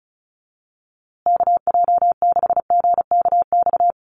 Morse code, heard as a keyed tone, about 34 wpm, 700 Hz. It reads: KJ6GKX
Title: USA Call Signs Repeated 3x 35wpm